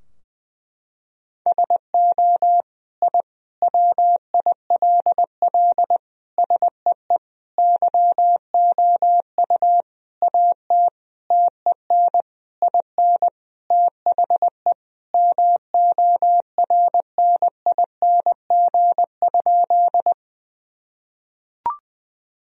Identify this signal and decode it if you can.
Morse code, heard as a keyed tone, about 21 wpm, 700 Hz. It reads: SO I WILL SEE YOU AT TEN IN THE MORNING?
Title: Sentences from Top 1000 Words - Review Part 2 - 20wpm